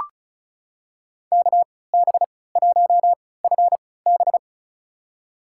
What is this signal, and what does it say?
Morse code, heard as a keyed tone, about 36 wpm, spaced as for 20 wpm, 700 Hz. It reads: KB1FB
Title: USA Call Signs - 5 Characters - 3x Character Spacing - 35wpm